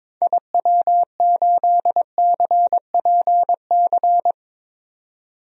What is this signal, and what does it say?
Morse code, heard as a keyed tone, about 22 wpm, 700 Hz. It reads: IW8CPC
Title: DXCC-100 Call Signs - 6 Characters - 22wpm